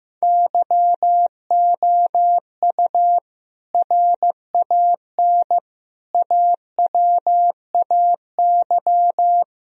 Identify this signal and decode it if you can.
Morse code, heard as a keyed tone, about 15 wpm, 700 Hz. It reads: YOU RAN AWAY
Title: Sentences from Top 1000 Words - Review Part 3 - 15wpm